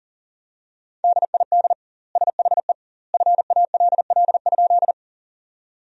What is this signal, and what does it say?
Morse code, heard as a keyed tone, about 40 wpm, 700 Hz. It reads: DID SHE FALL?
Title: 2x Word-Spacing - Sentences from Top 500 Words - 40wpm